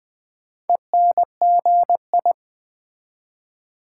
Morse code, heard as a keyed tone, about 20 wpm, 700 Hz. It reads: ENGI
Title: Build A Word - 4 Letters At A Time - 8-Letter Words - 20wpm